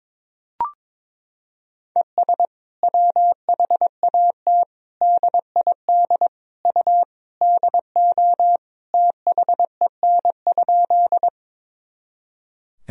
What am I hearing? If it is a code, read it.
Morse code, heard as a keyed tone, about 22 wpm, 700 Hz. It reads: ES WHAT DID U DO THEN?
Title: Sentences from Top 100 Words and Abbreviations 22wpm